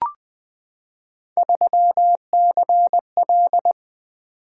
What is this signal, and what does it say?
Morse code, heard as a keyed tone, about 20 wpm, 700 Hz. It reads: 3CL
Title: Three Letter-Number Combinations - 20wpm